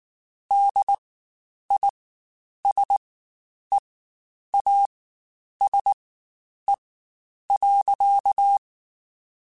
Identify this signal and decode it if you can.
Morse code, heard as a keyed tone, about 19 wpm, spaced as for 9 wpm, 800 Hz. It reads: DISEASE.
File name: Quotes Spaced / Quotes Spaced-mp3s / HoofInMouth sp @18